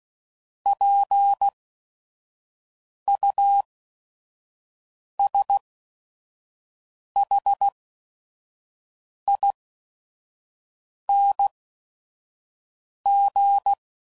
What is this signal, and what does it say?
Morse code, heard as a keyed tone, about 16 wpm, spaced as for 5 wpm, 800 Hz. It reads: PUSHING
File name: Suffixes / Special ING's / VerySlowToFaster ING's